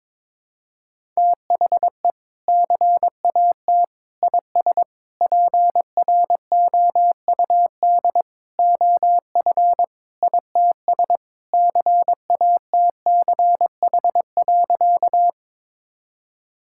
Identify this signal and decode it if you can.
Morse code, heard as a keyed tone, about 22 wpm, 700 Hz. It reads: THE CAT IS PROUD OF ITS CATCH.
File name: practice / 22wpm / Sentences with 2-Word Phrases - 22wpm